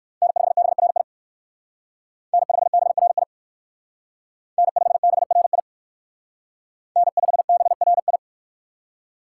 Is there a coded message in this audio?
Yes: Morse code from 0.2 to 8.2 s, heard as a keyed tone, about 64 wpm, 700 Hz. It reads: N5BRI N5BRI N5BRI N5BRI